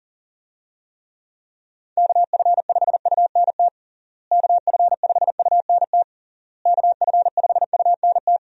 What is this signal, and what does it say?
Morse code, heard as a keyed tone, about 40 wpm, 700 Hz. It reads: KF5UNT KF5UNT KF5UNT
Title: USA Call Signs - 6 Characters - Repeated 3x - 40wpm